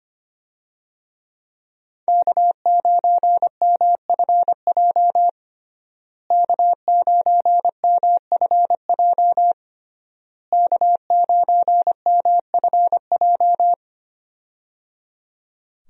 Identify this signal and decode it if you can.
Morse code, heard as a keyed tone, about 25 wpm, 700 Hz. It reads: K9MFJ K9MFJ K9MFJ
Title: USA Call Signs - 5 Characters - Repeated 3x - 25wpm